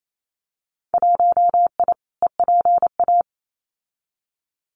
Morse code, heard as a keyed tone, about 28 wpm, 700 Hz. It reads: 1I EPA
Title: Winter Field Day Exchanges Encore 28wpm